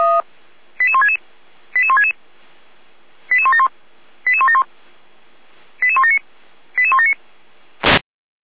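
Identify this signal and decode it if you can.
Five-tone selective call (ZVEI), heard as quick runs of five tones, tones of about 80 ms. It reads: 88166 88166 88171 88171 88179 88179
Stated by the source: ZVEI five-tone selective call received by SDR.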